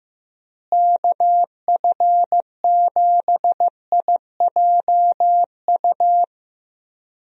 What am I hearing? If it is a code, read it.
Morse code, heard as a keyed tone, about 15 wpm, 700 Hz. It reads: KF7IJU
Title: USA Call Signs - 6 Characters - Repeated 3x - 15wpm